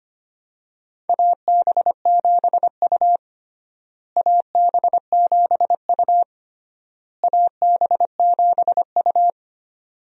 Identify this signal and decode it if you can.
Morse code, heard as a keyed tone, about 25 wpm, 700 Hz. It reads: AB7U AB7U AB7U
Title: USA Call Signs - 4 Characters - Repeated 3x - 25wpm